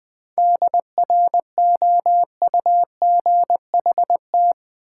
Morse code, heard as a keyed tone, about 20 wpm, 700 Hz. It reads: DROUGHT
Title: 7 Letter Words - First 3 Letters Given - 20wpm